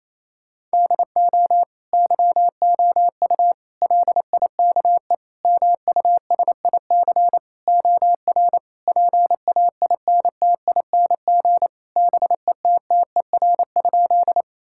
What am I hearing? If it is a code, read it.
Morse code, heard as a keyed tone, about 28 wpm, 700 Hz. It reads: DO YOU LIKE MUSIC OR PAINTING BETTER?
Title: Sentences from Top 1000 Words - Review Part 6 - 28wpm